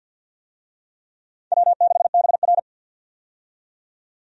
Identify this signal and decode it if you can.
Morse code, heard as a keyed tone, about 48 wpm, 700 Hz. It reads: W6BR
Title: USA Call Signs - 4 Characters - 50wpm